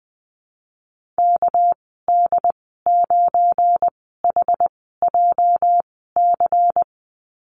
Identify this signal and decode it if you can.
Morse code, heard as a keyed tone, about 20 wpm, spaced as for 14 wpm, 700 Hz. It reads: KD9HJC
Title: 245 Lesson 39g - Letter z - Call Signs 20wpm